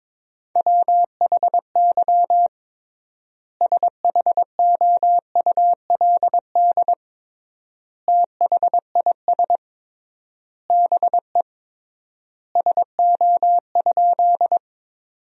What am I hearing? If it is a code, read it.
Morse code, heard as a keyed tone, about 22 wpm, 700 Hz. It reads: WHY SHOULD THIS BE SO?
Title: 3x Word-Spacing - Sentences from Top 300 Words - 22wpm